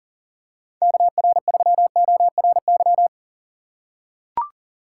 Morse code, heard as a keyed tone, about 37 wpm, 700 Hz. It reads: KR3ORY
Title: USA Call Signs - 6 Characters - 3x Character Spacing - 40wpm